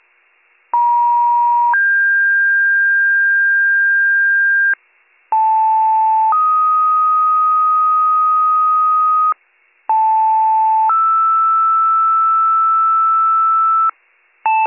Two-tone page, heard as a one-second tone, then a three-second tone, about 950 then 1600 Hz, about 880 then 1250 Hz, about 880 then 1320 Hz. It is heard from 0.7 to 13.9 s.